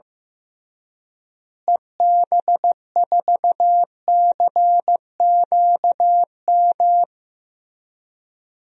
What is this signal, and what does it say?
Morse code, heard as a keyed tone, about 15 wpm, 700 Hz. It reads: EB4CQM
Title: DXCC-100 Call Signs - 6 Characters - Repeated 3x - 15wpm